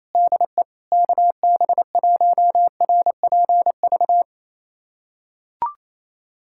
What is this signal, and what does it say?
Morse code, heard as a keyed tone, about 28 wpm, 700 Hz. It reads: DE KB1RPV
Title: USA Callsign DE Callsign - 6 Characters - 28wpm